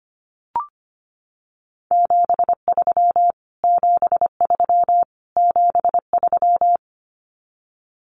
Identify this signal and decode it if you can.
Morse code, heard as a keyed tone, about 25 wpm, 700 Hz. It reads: 73 73 73
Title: Abbreviations Repeated 3x - 25wpm